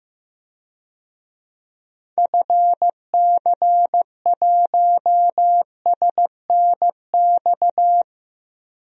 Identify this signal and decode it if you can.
Morse code, heard as a keyed tone, about 15 wpm, 700 Hz. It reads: FC1SNX
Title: DXCC-100 Call Signs - 6 Characters - 15wpm